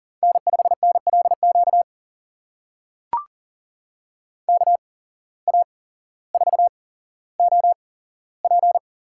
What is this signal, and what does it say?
Morse code, heard as a keyed tone, about 40 wpm, spaced as for 17 wpm, 700 Hz. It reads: N5NLQ KA4OP
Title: USA Call Signs - 8x Character Spacing - 40wpm